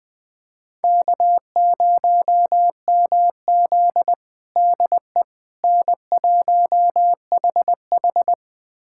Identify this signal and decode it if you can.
Morse code, heard as a keyed tone, about 20 wpm, 700 Hz. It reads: K0MZ DE N1HH
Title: USA Callsign DE Callsign - 4 Characters - 20wpm